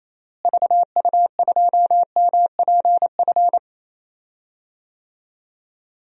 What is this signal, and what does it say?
Morse code, heard as a keyed tone, about 27 wpm, 700 Hz. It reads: VU2MPF
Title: DXCC-100 Call Signs - 6 Characters - Repeated 2x - 28wpm